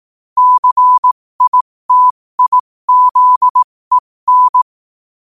Morse code, heard as a keyed tone, about 18 wpm, spaced as for 15 wpm, 1000 Hz. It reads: CITIZEN
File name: qotdfiles / 15wpm